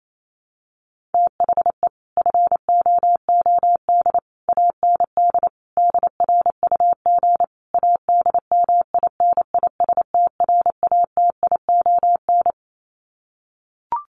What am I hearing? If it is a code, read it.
Morse code, heard as a keyed tone, about 28 wpm, 700 Hz. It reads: THE FOOD AND DRUG ADMINISTRATION
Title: Sets of 5 Words 28wpm